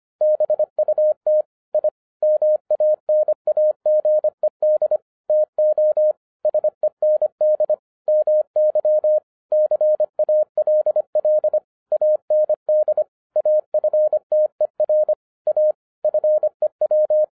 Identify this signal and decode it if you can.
Morse code, heard as a keyed tone, about 25 wpm, 610 Hz. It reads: BUT I MANAGED TO SEND MY CALL AND AFTER A FEW
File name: FIRST FISTS CONTACT-25wpm